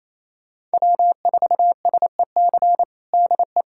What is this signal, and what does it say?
Morse code, heard as a keyed tone, about 27 wpm, 700 Hz. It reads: W4SEC DE
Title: USA Callsign DE Callsign - 5 Characters - 28wpm